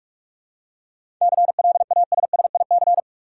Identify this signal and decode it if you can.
Morse code, heard as a keyed tone, about 46 wpm, 700 Hz. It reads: CLASSIC
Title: Seven Letter Words - 45wpm